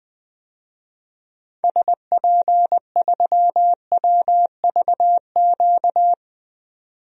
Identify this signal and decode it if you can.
Morse code, heard as a keyed tone, about 20 wpm, 700 Hz. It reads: SP3WVQ
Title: DXCC-100 Call Signs - 6 Characters - 20wpm